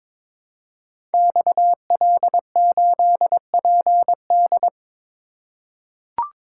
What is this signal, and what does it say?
Morse code, heard as a keyed tone, about 22 wpm, 700 Hz. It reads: XL8PD
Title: Code Groups - 5 Letter:Numbers - 22wpm